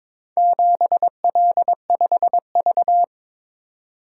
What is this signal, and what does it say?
Morse code, heard as a keyed tone, about 22 wpm, 700 Hz. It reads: 7L5V
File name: practice / 22wpm / Four Letter-Number Combinations - 22wpm